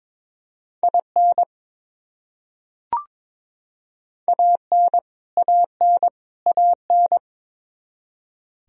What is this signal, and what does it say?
Morse code, heard as a keyed tone, about 21 wpm, 700 Hz. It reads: IN AN AN AN
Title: Top 100 Words Repeated 22wpm